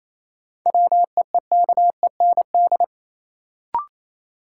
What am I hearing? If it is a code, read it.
Morse code, heard as a keyed tone, about 28 wpm, 700 Hz. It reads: WEEKEND
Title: Sets of 5 Words - New Word Drill - 28wpm